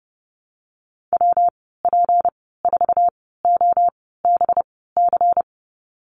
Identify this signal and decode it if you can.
Morse code, heard as a keyed tone, about 30 wpm, spaced as for 17 wpm, 700 Hz. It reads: WP4OBC
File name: Lesson 37g - Letter x - Call Signs 30wpm